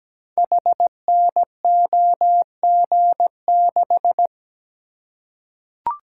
Morse code, heard as a keyed tone, about 17 wpm, 700 Hz. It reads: HNOG6
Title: Code Groups - 5 Letter:Numbers - 17wpm